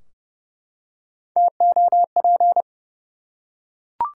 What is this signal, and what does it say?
Morse code, heard as a keyed tone, about 31 wpm, 700 Hz. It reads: TOP
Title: Three Letter Words - 30wpm